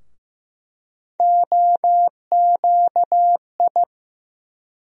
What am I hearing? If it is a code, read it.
Morse code, heard as a keyed tone, about 15 wpm, 700 Hz. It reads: OQI
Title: USA Call Sign Postfixes - 3 Letters 15wpm